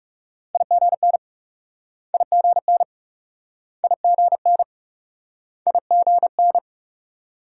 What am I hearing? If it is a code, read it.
Morse code, heard as a keyed tone, about 41 wpm, 700 Hz. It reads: IGN IGN IGN IGN